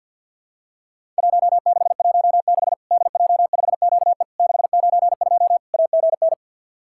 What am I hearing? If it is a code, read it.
Morse code, heard as a keyed tone, about 50 wpm, 700 Hz. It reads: 1616 DJ5QE 692 AGN
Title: Worked All Europe DX Contest - QTC Exchange with Repeats - 50wpm